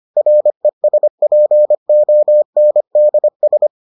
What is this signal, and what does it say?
Morse code, heard as a keyed tone, about 25 wpm, 600 Hz. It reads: RESPONDS